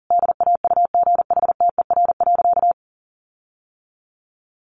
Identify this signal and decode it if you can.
Morse code, heard as a keyed tone, about 40 wpm, 700 Hz. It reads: DAUGHTER.
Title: Sentences from Top 1500 Words 40wpm